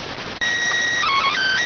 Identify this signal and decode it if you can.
SSTV (slow-scan television), PD 290.